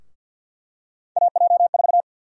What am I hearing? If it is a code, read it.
Morse code, heard as a keyed tone, about 47 wpm, 700 Hz. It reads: AJ4
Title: USA Call Sign Prefixes - 50wpm